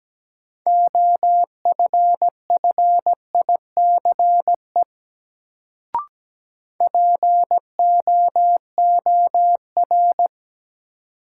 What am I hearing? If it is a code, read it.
Morse code, heard as a keyed tone, about 17 wpm, 700 Hz. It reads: OFFICE POOR
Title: Sentences from Top 1000 Words - New Word Drill - 17wpm